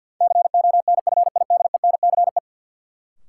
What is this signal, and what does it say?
Morse code, heard as a keyed tone, about 51 wpm, 700 Hz. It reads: CONFIDENCE
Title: Sentences from Top 1500 Words - New Word Drill Alt - 50wpm